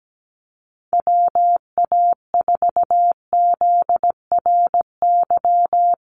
Morse code, heard as a keyed tone, about 17 wpm, 700 Hz. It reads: WA4ZRY
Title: USA Call Signs 17wpm